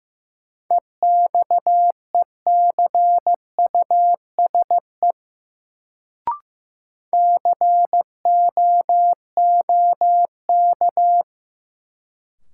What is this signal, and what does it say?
Morse code, heard as a keyed tone, about 15 wpm, 700 Hz. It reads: EXECUSE COOK
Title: Sentences from Top 700 Words - New Word Drill - 15wpm